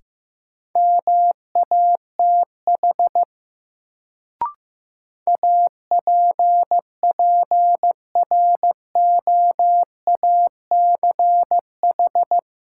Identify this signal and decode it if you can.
Morse code, heard as a keyed tone, about 15 wpm, 700 Hz. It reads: MATH APPROACH
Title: Sentences from Top 1000 Words - Review Part 6 - New Word Drill - 15wpm